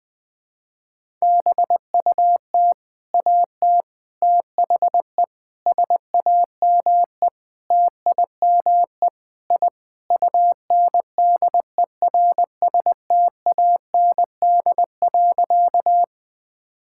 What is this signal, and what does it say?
Morse code, heard as a keyed tone, about 20 wpm, 700 Hz. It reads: BUT AT THE SAME TIME I UNDERSTAND.
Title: Sentences with 4-Word Phrases - Spoken Beginning - 20wpm